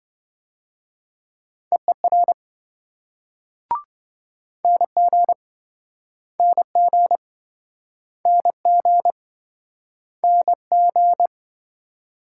Morse code, heard as a keyed tone, about 27 wpm, 700 Hz. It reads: EER NG NG NG NG